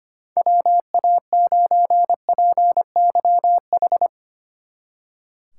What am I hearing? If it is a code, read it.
Morse code, heard as a keyed tone, about 31 wpm, 700 Hz. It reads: WA9PYH